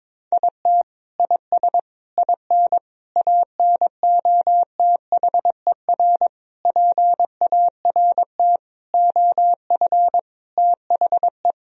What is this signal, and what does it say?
Morse code, heard as a keyed tone, about 22 wpm, 700 Hz. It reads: IT IS IN ANOTHER PART OF THE
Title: Sentences from Top 1000 Words - Review Part 2 - 22wpm